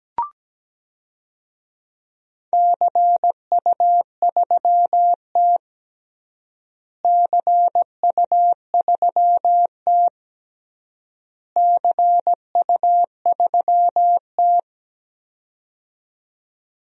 Morse code, heard as a keyed tone, about 17 wpm, 700 Hz. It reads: CU3T CU3T CU3T
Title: DXCC-100 Call Signs - 4 Characters - Repeated 3x - 17wpm